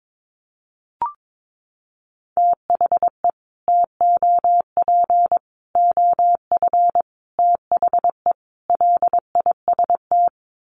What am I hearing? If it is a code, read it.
Morse code, heard as a keyed tone, about 22 wpm, 700 Hz. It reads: THE TOP OF THE LIST
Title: Sets of 5 Words Encore2 22wpm